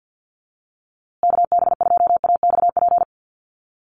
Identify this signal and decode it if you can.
Morse code, heard as a keyed tone, about 49 wpm, 700 Hz. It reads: KB2AXP
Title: USA Call Signs Repeated 2x 50wpm